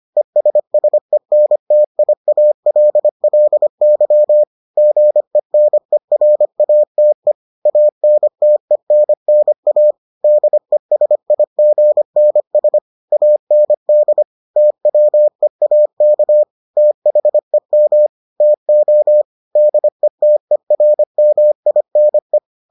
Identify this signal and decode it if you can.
Morse code, heard as a keyed tone, about 25 wpm, 600 Hz. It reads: ESSENTIALLY GENERATE ANTENNA DESIGNS AND TWEAK THEM TO DETERMINE